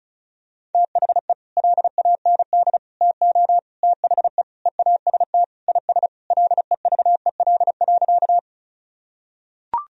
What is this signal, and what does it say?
Morse code, heard as a keyed tone, about 34 wpm, 700 Hz. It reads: THE LAND TO THE EAST IS LEVEL.
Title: Sentences from Top 1000 Words - Review Part 4 - 35wpm